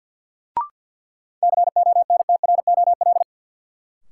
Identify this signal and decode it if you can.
Morse code, heard as a keyed tone, about 53 wpm, 700 Hz. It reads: CONTROL
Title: Sentences from Top 1000 Words - Review Part 2 - New Word Drill - 50wpm